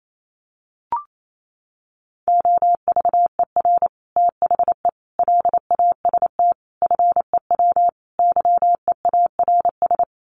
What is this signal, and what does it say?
Morse code, heard as a keyed tone, about 28 wpm, 700 Hz. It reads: OVER THE LAST FEW YEARS
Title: Sets of 5 Words Encore 28wpm